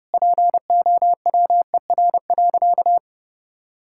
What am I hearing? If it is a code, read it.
Morse code, heard as a keyed tone, about 30 wpm, 700 Hz. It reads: POWER.
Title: Sentences from Top 1000 Words - Review Part 5 - 30wpm